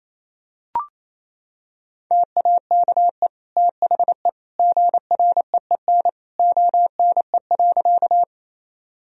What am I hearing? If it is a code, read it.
Morse code, heard as a keyed tone, about 27 wpm, 700 Hz. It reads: TAKE THE GREEN ONE.
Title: Sentences from Top 1000 Words - Review Part 2 - 28wpm